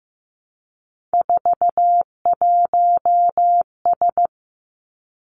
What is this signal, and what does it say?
Morse code, heard as a keyed tone, about 15 wpm, 700 Hz. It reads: 41S
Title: Three Challenging Letter-Number Combinations 15wpm